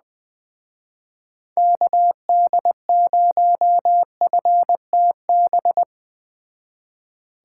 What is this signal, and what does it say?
Morse code, heard as a keyed tone, about 20 wpm, 700 Hz. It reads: KD0FTB
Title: Speed Racing - USA 2x3 Call Signs 17wpm